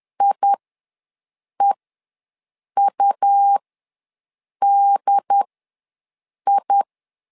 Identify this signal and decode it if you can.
Morse code, heard as a keyed tone, about 10 wpm, spaced as for 6 wpm, 800 Hz. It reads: IEUDI
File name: LEZIONI / 2 UHDN / 2 UHDN ws 10wpm 2044